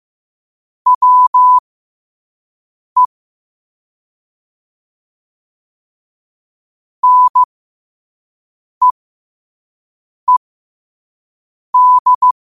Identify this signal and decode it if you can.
Morse code, heard as a keyed tone, about 15 wpm, spaced as for 5 wpm, 1000 Hz. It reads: WE NEED
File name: 5wpm